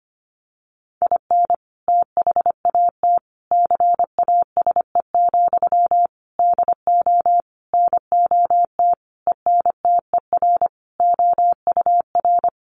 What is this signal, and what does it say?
Morse code, heard as a keyed tone, about 25 wpm, 700 Hz. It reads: IN THAT CASE, DO NOT ENTER OUR
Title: Sentences from Top 1300 Words 25wpm